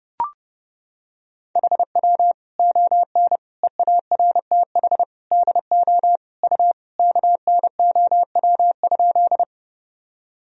Morse code, heard as a keyed tone, about 32 wpm, 700 Hz. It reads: HW ON EARTH DO U KNOW?
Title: Sentences from Top 200 Words and Abbreviations 30wpm